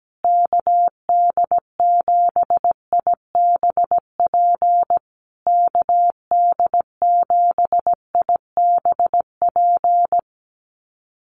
Morse code, heard as a keyed tone, about 17 wpm, 700 Hz. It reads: KD7IBP KD7IBP
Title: USA Call Signs Repeated 3x 17wpm